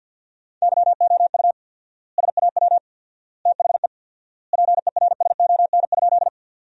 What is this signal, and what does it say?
Morse code, heard as a keyed tone, about 50 wpm, 700 Hz. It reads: YOU SAW THE PERSON?
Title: 4x Word-Spacing - Sentences from Top 400 Words - 50wpm